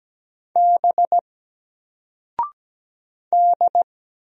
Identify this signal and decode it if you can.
Morse code, heard as a keyed tone, about 17 wpm, 700 Hz. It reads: B D
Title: Single Character (Letter:Number:Punctuation:Prosign) 17wpm